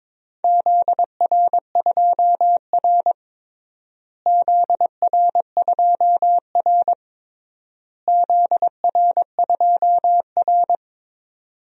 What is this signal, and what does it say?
Morse code, heard as a keyed tone, about 22 wpm, 700 Hz. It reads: ZR2R ZR2R ZR2R
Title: DXCC-100 Call Signs - 4 Characters - Repeated 3x - 22wpm